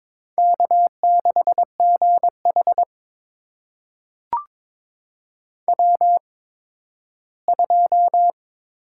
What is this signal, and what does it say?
Morse code, heard as a keyed tone, about 22 wpm, spaced as for 13 wpm, 700 Hz. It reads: K6GH W2
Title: USA Call Signs - 4 Characters - 8x Character Spacing - 22wpm